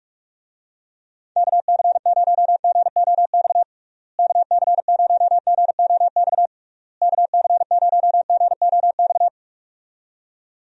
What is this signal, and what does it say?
Morse code, heard as a keyed tone, about 44 wpm, 700 Hz. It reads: KC0GOX KC0GOX KC0GOX
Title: USA Call Signs - 6 Characters - Repeated 3x - 45wpm